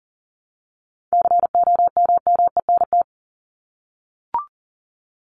Morse code, heard as a keyed tone, about 40 wpm, 700 Hz. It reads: COMMENT
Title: Top 1751-2000 Words 40wpm